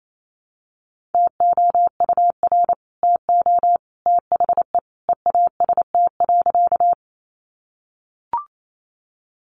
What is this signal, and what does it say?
Morse code, heard as a keyed tone, about 28 wpm, 700 Hz. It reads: TOUR TO THE EAST.
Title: Sentences from Top 1400 Words 28wpm